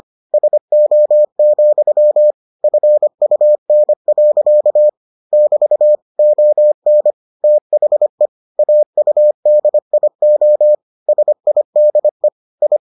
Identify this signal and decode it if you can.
Morse code, heard as a keyed tone, about 25 wpm, 600 Hz. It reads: SO, FUN. <BT> ON THE AUDIO SIDE I